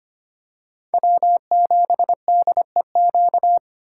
Morse code, heard as a keyed tone, about 25 wpm, 700 Hz. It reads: W7DEQ
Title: USA Call Signs - 5 Characters - 3x Character Spacing - 25wpm